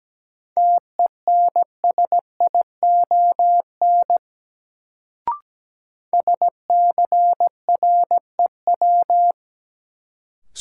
Morse code, heard as a keyed tone, about 17 wpm, 700 Hz. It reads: TENSION SCREW
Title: Sentences from Top 1600 Words - New Word Drill Alt - 17wpm